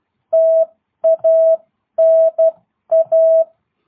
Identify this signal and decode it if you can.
Morse code, heard as a keyed tone, about 12 wpm, spaced as for 10 wpm, 660 Hz. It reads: TANA
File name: jh8cbh / 2022cw